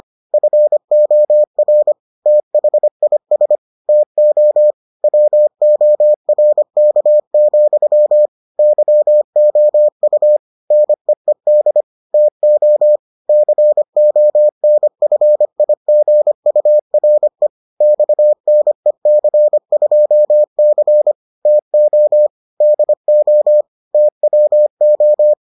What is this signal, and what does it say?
Morse code, heard as a keyed tone, about 25 wpm, 600 Hz. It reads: FOR THIS TO WORK, YOU NEED TO CONFIGURE XNEC2C TO DO TWO